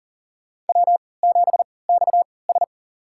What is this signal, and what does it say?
Morse code, heard as a keyed tone, about 40 wpm, spaced as for 23 wpm, 700 Hz. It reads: W7XS
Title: USA Call Signs - 4 Characters - 3x Character Spacing - 40wpm